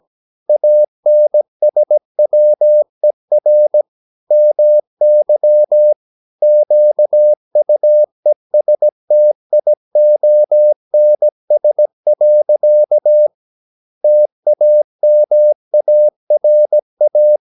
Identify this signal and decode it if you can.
Morse code, heard as a keyed tone, about 17 wpm, 600 Hz. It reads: ANSWER MY QUESTIONS. TAMARA
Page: Romanen Never af Ken Follet på engelsk.